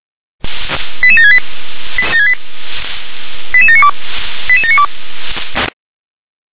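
Five-tone selective call (ZVEI), heard as quick runs of five tones, tones of about 70 ms. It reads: 88768 88792 88792
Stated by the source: Received on an SDR as a ZVEI five-tone call.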